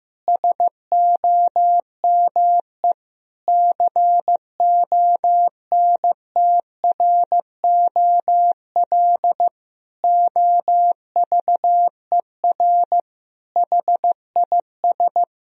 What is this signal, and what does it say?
Morse code, heard as a keyed tone, about 15 wpm, 700 Hz. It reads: SOME CONTROL OVER HIS
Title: Sentences from Top 1000 Words - Review Part 3 - 15wpm